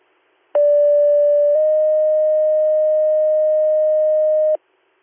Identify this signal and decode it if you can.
Two-tone page, heard as a one-second tone, then a three-second tone, about 580 then 620 Hz. It is heard from 0.5 to 4.6 s.